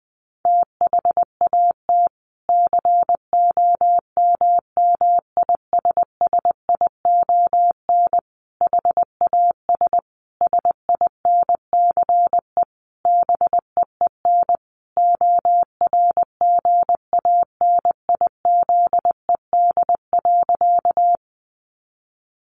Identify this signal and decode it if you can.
Morse code, heard as a keyed tone, about 20 wpm, 700 Hz. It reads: THAT COMMISSION HAS SINCE BEEN ORGANIZED.
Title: Sentences from Top 1400 Words 20wpm